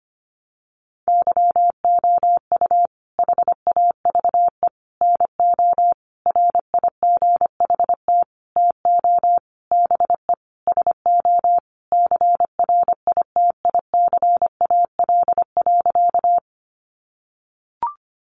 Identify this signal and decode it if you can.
Morse code, heard as a keyed tone, about 25 wpm, 700 Hz. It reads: YOU HAVE NO RIGHT TO BE SO CRITICAL.
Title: Sentences from Top 1100 Words 25wpm